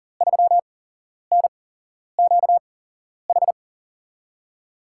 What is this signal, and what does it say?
Morse code, heard as a keyed tone, about 39 wpm, spaced as for 11 wpm, 700 Hz. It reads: 3NQH
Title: USA Call Signs - 6 Characters - 8x Character Spacing - 40wpm